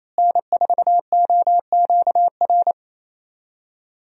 Morse code, heard as a keyed tone, about 28 wpm, 700 Hz. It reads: N4OQR
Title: USA Call Signs - 5 Characters - Repeated 2x - 28wpm